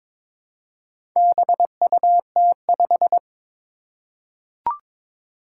Morse code, heard as a keyed tone, about 21 wpm, 700 Hz. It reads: BUT5
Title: Four Challenging Letter-Number Combinations 22wpm